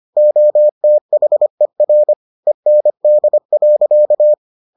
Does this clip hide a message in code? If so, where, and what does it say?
Yes: Morse code from 0.2 to 4.3 s, heard as a keyed tone, about 25 wpm, 600 Hz. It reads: OTHER END.